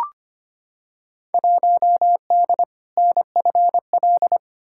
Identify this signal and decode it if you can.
Morse code, heard as a keyed tone, about 25 wpm, 700 Hz. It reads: 1D NFL
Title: Field Day Exchanges - 25wpm